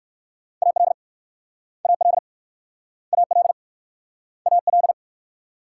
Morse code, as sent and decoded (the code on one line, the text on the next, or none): .- .-.. / .- .-.. / .- .-.. / .- .-..
AL AL AL AL